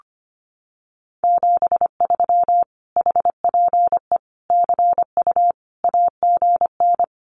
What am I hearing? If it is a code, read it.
Morse code, heard as a keyed tone, about 25 wpm, 700 Hz. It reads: 73 HPE CU AGN
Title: Common QSO Phrases Encore 25wpm